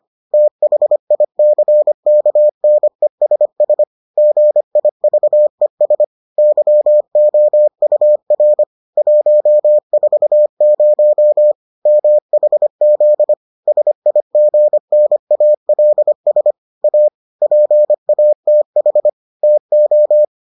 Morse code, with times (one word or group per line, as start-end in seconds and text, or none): 0.3-3.8 THICKNESS
4.2-6.0 GIVES
6.4-8.6 YOUR
9.0-11.5 140
11.9-13.3 MHZ
13.7-16.5 SIGNALS
16.8-17.1 A
17.4-19.1 PATH
19.4-20.2 TO